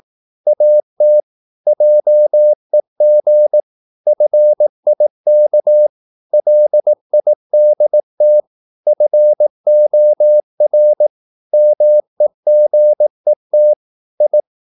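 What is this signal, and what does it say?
Morse code, heard as a keyed tone, about 18 wpm, 600 Hz. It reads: AT JEG FIK LIDT FOR MEGET I
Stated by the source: Slow (QRS) Morse practice at 18 wpm, CW med Gnister.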